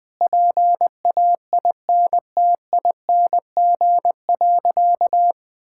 Morse code, heard as a keyed tone, about 20 wpm, 700 Hz. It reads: PAINTING.
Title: Sentences from Top 1000 Words - Review Part 6 - 20wpm